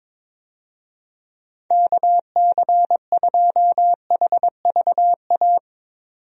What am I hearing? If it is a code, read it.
Morse code, heard as a keyed tone, about 22 wpm, 700 Hz. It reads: KC2HVA
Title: USA Call Signs - 6 Characters - 22wpm